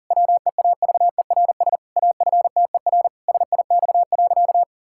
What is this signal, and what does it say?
Morse code, heard as a keyed tone, about 37 wpm, 700 Hz. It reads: WEAVERS AFTER SIX.
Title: Pangram Challenge v2 - 40wpm